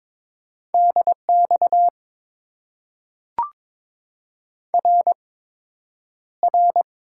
Morse code, heard as a keyed tone, about 22 wpm, spaced as for 9 wpm, 700 Hz. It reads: DX RR
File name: Top 90 QSO Elements - 8x Character Spacing - 22wpm